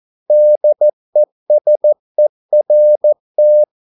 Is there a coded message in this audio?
Yes: Morse code from 0.3 to 3.6 s, heard as a keyed tone, about 14 wpm, 600 Hz. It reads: DESERT